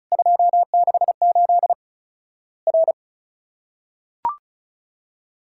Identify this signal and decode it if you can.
Morse code, heard as a keyed tone, about 36 wpm, 700 Hz. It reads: 268 R
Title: Worked All Europe DX Contest - QTC Exchange - 35wpm